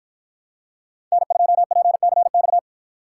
Morse code, heard as a keyed tone, about 53 wpm, 700 Hz. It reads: N2PCX